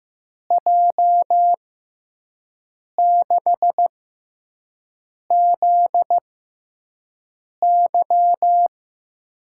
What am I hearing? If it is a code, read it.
Morse code, heard as a keyed tone, about 15 wpm, spaced as for 5 wpm, 700 Hz. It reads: J6ZY
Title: USA Call Signs - 5 Characters - 6x Character Spacing - 15wpm